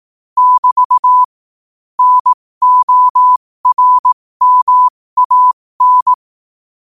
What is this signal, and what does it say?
Morse code, heard as a keyed tone, about 18 wpm, spaced as for 15 wpm, 1000 Hz. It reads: <BT> NORMAN
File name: qotdfiles / 15wpm